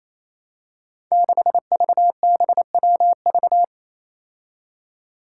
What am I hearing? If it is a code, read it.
Morse code, heard as a keyed tone, about 28 wpm, 700 Hz. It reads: 6VBWV
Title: Five Challenging Letter-Number Combinations 28wpm